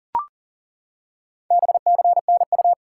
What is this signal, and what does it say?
Morse code, heard as a keyed tone, about 40 wpm, 700 Hz. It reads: BCNU
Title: Top 90 QSO Elements 40wpm